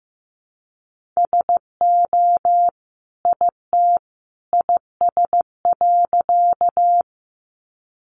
Morse code, heard as a keyed tone, about 15 wpm, 700 Hz. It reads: SO IT IS.
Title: Sentences from Top 100 Words 15wpm